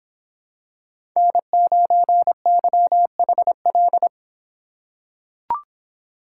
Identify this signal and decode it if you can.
Morse code, heard as a keyed tone, about 25 wpm, 700 Hz. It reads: N9YHL